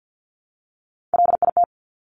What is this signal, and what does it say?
Morse code, heard as a keyed tone, about 50 wpm, 690 Hz. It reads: LIT
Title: Top Three Letter Combinations in Words 50wpm